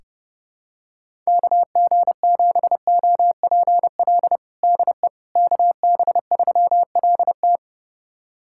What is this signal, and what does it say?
Morse code, heard as a keyed tone, about 30 wpm, 700 Hz. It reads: KG7OPL DE KB3LT
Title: USA Callsign DE Callsign 30wpm